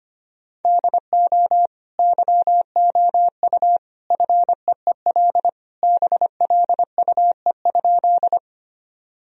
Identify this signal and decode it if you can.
Morse code, heard as a keyed tone, about 25 wpm, 700 Hz. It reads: DO YOU FEEL BLUE?
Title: Sentences from Top 1000 Words - Review Part 3 - 25wpm